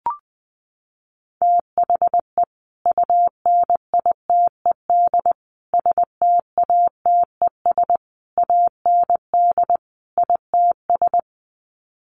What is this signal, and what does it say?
Morse code, heard as a keyed tone, about 19 wpm, 700 Hz. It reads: THE UNITED STATES AND ITS
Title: Sets of 5 Words Encore 20wpm